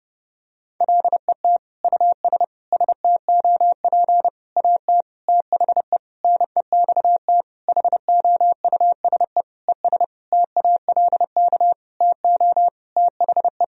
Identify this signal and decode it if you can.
Morse code, heard as a keyed tone, about 32 wpm, 700 Hz. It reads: LET US STOP AT THE NEXT HOUSE ES TALK TO THE
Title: Sentences from Top 400 Words and Abbreviations 30wpm